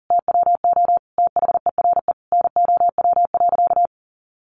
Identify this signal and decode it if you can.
Morse code, heard as a keyed tone, about 40 wpm, 700 Hz. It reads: TWO THERE NOW.
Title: Sentences from Top 200 Words 40wpm